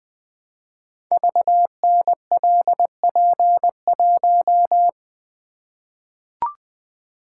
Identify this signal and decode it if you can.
Morse code, heard as a keyed tone, about 20 wpm, 700 Hz. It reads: VNLP1
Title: Code Groups - 5 Letter:Numbers - 20wpm